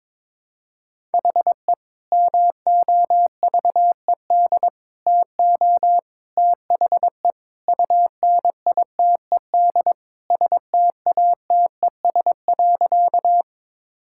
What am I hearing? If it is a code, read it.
Morse code, heard as a keyed tone, about 21 wpm, 700 Hz. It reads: HE MOVED TO THE UNITED STATES.
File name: practice / 22wpm / Sentences with 3-Word Phrases - 22wpm